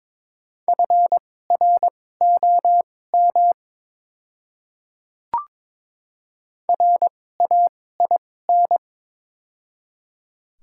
Morse code, heard as a keyed tone, about 22 wpm, spaced as for 16 wpm, 700 Hz. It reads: FROM RAIN
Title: 2x Character Spacing - Top 500 Words - 22wpm